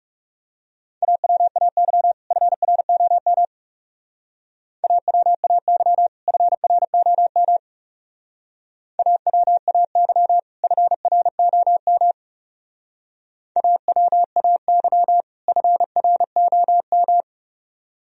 Morse code, heard as a keyed tone, about 34 wpm, 700 Hz. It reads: AWAY FROM AWAY FROM AWAY FROM AWAY FROM
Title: Speed-Racing - Sets of 2 Words 30wpm